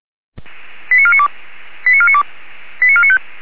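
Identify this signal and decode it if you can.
Five-tone selective call (ZVEI), heard as quick runs of five tones, tones of about 70 ms. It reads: 89492 89492 89495
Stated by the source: ZVEI five-tone selective call received on SDR.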